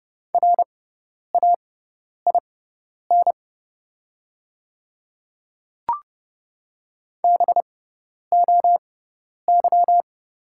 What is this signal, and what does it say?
Morse code, heard as a keyed tone, about 30 wpm, spaced as for 10 wpm, 700 Hz. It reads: RAIN BOY
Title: 6x Character Spacing - Top 500 Words - 30wpm